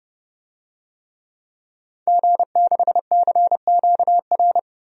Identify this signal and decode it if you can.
Morse code, heard as a keyed tone, about 31 wpm, 690 Hz. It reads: G6CQR
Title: DXCC-100 Call Signs - 5 Characters - 30wpm